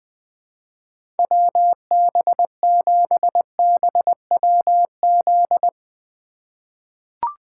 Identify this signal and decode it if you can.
Morse code, heard as a keyed tone, about 20 wpm, 700 Hz. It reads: WB7BWZ
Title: 2023 SOTA Chaser Call Signs - 20wpm